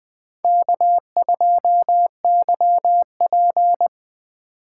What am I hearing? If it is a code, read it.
Morse code, heard as a keyed tone, about 20 wpm, 700 Hz. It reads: K2YP
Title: USA Call Signs - 4 Characters - 4x Character Spacing - 20wpm